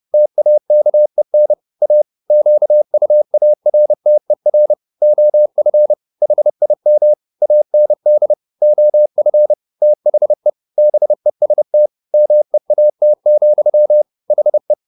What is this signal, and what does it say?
Morse code, heard as a keyed tone, about 30 wpm, 600 Hz. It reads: TAKEN A QUARTER OF HIM AND OF THE BEST MEAT, HE